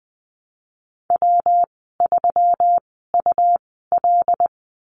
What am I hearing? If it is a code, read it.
Morse code, heard as a keyed tone, about 20 wpm, spaced as for 14 wpm, 700 Hz. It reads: W3UL
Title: 167 Lesson 27g - Number 7 - Call Signs 20wpm